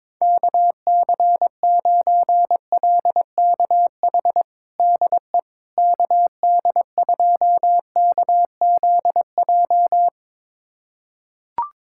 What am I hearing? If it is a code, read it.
Morse code, heard as a keyed tone, about 22 wpm, 700 Hz. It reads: KC9LKH DE KD2KZJ
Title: USA Callsign DE Callsign 22wpm